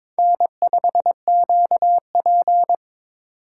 Morse code, heard as a keyed tone, about 26 wpm, 700 Hz. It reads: N5QP
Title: Speed Racing - USA 1x2 Call Signs 22wpm